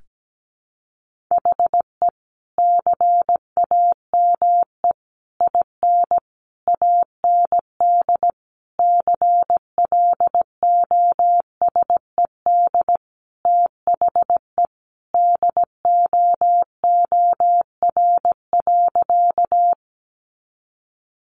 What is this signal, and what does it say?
Morse code, heard as a keyed tone, about 17 wpm, 700 Hz. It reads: HE CAME IN AND CLOSED THE DOOR.
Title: Sentences from Top 1400 Words 17wpm